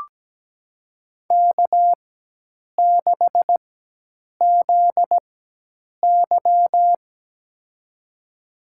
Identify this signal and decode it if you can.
Morse code, heard as a keyed tone, about 17 wpm, spaced as for 8 wpm, 700 Hz. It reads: K6ZY
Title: USA Call Signs - 4 Characters - 4x Character Spacing - 17wpm